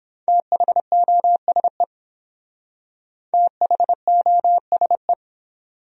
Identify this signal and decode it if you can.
Morse code, heard as a keyed tone, about 29 wpm, 700 Hz. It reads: THOSE THOSE